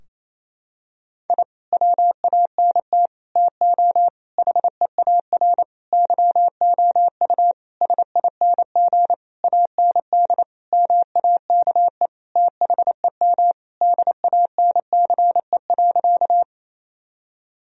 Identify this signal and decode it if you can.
Morse code, heard as a keyed tone, about 28 wpm, 700 Hz. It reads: I WANT TO HEAR YOU SING AND MAKE THEM DANCE.
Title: Sentences from Top 1000 Words - Review Part 4 - 28wpm